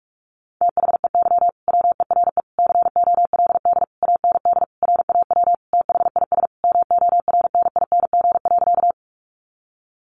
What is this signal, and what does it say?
Morse code, heard as a keyed tone, about 44 wpm, 700 Hz. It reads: THEY WERE COLD AND RAW THIS MORNING.
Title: Sentences from Top 1500 Words 45wpm